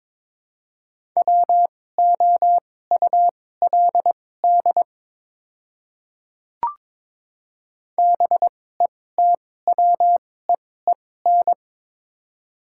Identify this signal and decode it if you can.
Morse code, heard as a keyed tone, about 22 wpm, spaced as for 16 wpm, 700 Hz. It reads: WOULD BETWEEN
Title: 2x Character Spacing - Top 300 Words - 22wpm